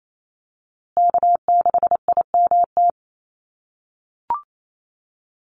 Morse code, heard as a keyed tone, about 28 wpm, 700 Hz. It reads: K6IMT
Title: USA Call Signs Repeated 3x 28wpm